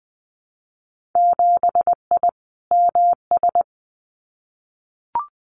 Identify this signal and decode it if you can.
Morse code, heard as a keyed tone, about 19 wpm, 700 Hz. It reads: 7I MS
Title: Winter Field Day Exchanges Encore 20wpm